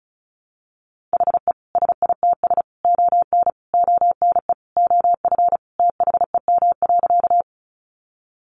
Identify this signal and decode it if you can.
Morse code, heard as a keyed tone, about 34 wpm, 700 Hz. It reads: HE SITS ON ONE OF THEM.
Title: Additional Easy Phrases 35wpm